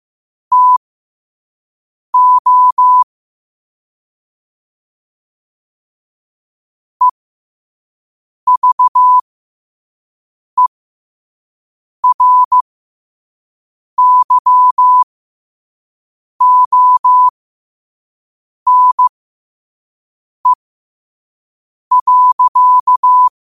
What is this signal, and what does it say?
Morse code, heard as a keyed tone, about 15 wpm, spaced as for 5 wpm, 1000 Hz. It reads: TO EVERYONE.
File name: qotdfiles / 5wpm